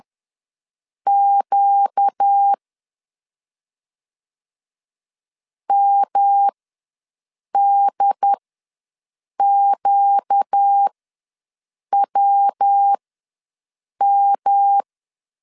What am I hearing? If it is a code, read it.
Morse code, heard as a keyed tone, about 10 wpm, spaced as for 6 wpm, 800 Hz. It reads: Q MDQWM
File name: LEZIONI / 5 LBVC / 5 LBV ws 10wpm 3645